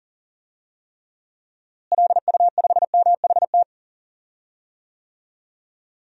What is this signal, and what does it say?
Morse code, heard as a keyed tone, about 37 wpm, 700 Hz. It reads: LU5MHT
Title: DXCC-100 Call Signs - 6 Characters - Repeated 2x - 40wpm